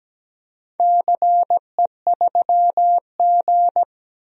Morse code, heard as a keyed tone, about 17 wpm, 700 Hz. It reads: CE3G
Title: DXCC-100 Call Signs - 4 Characters - 17wpm